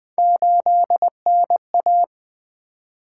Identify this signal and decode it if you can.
Morse code, heard as a keyed tone, about 20 wpm, 700 Hz. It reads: 8NA
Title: Three Letter-Number Combinations - 20wpm